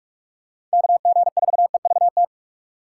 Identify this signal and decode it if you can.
Morse code, heard as a keyed tone, about 41 wpm, 700 Hz. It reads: KG4EVT
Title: USA Call Signs - 6 Characters - 6x Character Spacing - 45wpm